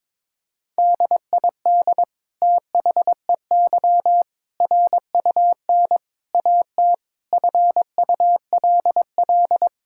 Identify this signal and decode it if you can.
Morse code, heard as a keyed tone, about 22 wpm, 700 Hz. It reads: DID THEY RUN AT FULL
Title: Sentences from Top 1000 Words - Review Part 5 - 22wpm